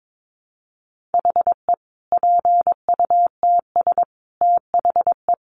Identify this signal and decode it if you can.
Morse code, heard as a keyed tone, about 22 wpm, 700 Hz. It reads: HE PUTS THE
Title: Additional Easy Phrases 22wpm